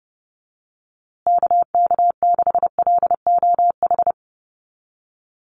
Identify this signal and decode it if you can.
Morse code, heard as a keyed tone, about 30 wpm, 700 Hz. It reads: KK6LOH
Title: USA Call Signs Repeated 2x 30wpm